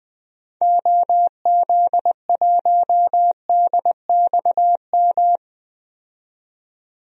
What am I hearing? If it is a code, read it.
Morse code, heard as a keyed tone, about 20 wpm, 700 Hz. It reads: OZ1DXM
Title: DXCC-100 Call Signs - 6 Characters - 20wpm